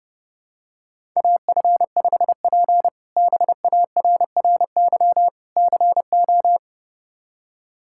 Morse code, heard as a keyed tone, about 32 wpm, 700 Hz. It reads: AF5P BARRY CO
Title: K1USN SST QSO Practice - 30wpm